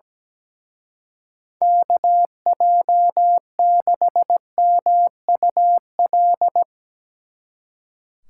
Morse code, heard as a keyed tone, about 18 wpm, 700 Hz. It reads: KJ6MUL
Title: Speed Racing - USA 2x3 Call Signs 17wpm